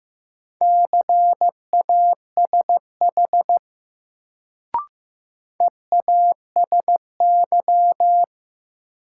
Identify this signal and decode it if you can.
Morse code, heard as a keyed tone, about 15 wpm, 700 Hz. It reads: CASH EASY
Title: Sentences from Top 700 Words - New Word Drill - 15wpm